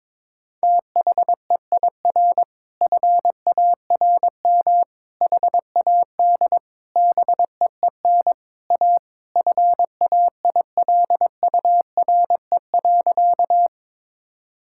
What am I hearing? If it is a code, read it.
Morse code, heard as a keyed tone, about 22 wpm, 700 Hz. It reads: THEIR FARM HAD BEEN A FAILURE.
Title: Sentences from Top 1000 Words - Review Part 6 - 22wpm